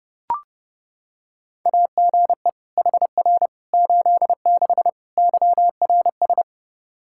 Morse code, heard as a keyed tone, about 30 wpm, 700 Hz. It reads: AGE HR 86 YRS
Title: Practice QSO - Exchange 3 - More Info - 30wpm